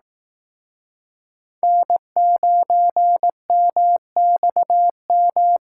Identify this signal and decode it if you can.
Morse code, heard as a keyed tone, about 19 wpm, 700 Hz. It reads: N9MXM